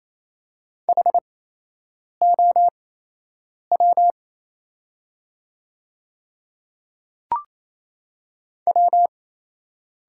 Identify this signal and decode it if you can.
Morse code, heard as a keyed tone, about 27 wpm, spaced as for 8 wpm, 690 Hz. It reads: HOW W
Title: 8x Character Spacing - Top 200 Words - 28wpm